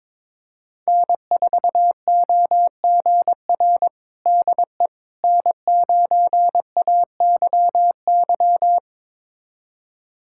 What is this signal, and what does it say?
Morse code, heard as a keyed tone, about 22 wpm, 700 Hz. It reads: N4OGR DE N9AYY
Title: USA Callsign DE Callsign - 5 Characters - 22wpm